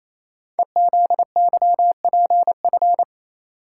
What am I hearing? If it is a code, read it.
Morse code, heard as a keyed tone, about 28 wpm, 700 Hz. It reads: EZYPF
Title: Code Groups - 5 Letters - 28wpm